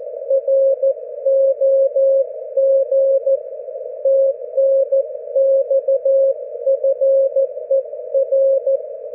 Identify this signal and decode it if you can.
Morse code, heard as a keyed tone, about 14 wpm, 540 Hz. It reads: ROG TNX FER
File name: GW3UEP / GW4HXO 40mWRF 42mls 24Jly2012